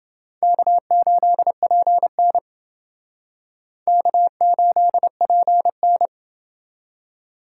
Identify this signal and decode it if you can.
Morse code, heard as a keyed tone, about 28 wpm, 700 Hz. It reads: K8PN K8PN